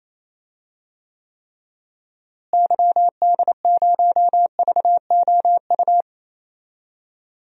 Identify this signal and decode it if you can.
Morse code, heard as a keyed tone, about 28 wpm, 700 Hz. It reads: YD0VOU
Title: DXCC-100 Call Signs - 6 Characters - Repeated 3x - 28wpm